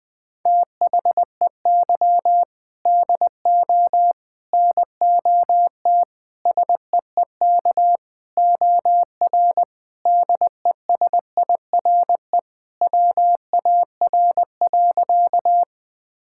Morse code, heard as a keyed tone, about 20 wpm, 700 Hz. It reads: THEY DO NOT SEEK OR DESIRE WAR.
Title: Sentences from Top 1600 Words 20wpm